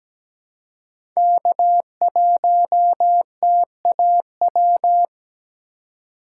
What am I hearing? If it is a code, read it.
Morse code, heard as a keyed tone, about 17 wpm, 700 Hz. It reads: K1TAW
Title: USA Call Signs - 3x Character Spacing - 17wpm